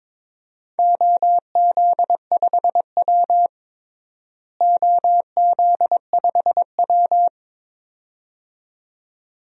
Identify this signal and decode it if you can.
Morse code, heard as a keyed tone, about 22 wpm, 700 Hz. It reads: OZ5W OZ5W
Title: DXCC-100 Call Signs - 4 Characters - Repeated 2x - 22wpm